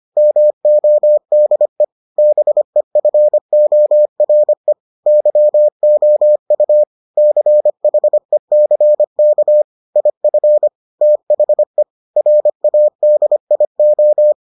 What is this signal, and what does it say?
Morse code, heard as a keyed tone, about 25 wpm, 600 Hz. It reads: MODE BEFORE YOU CHECK IF THE RADIO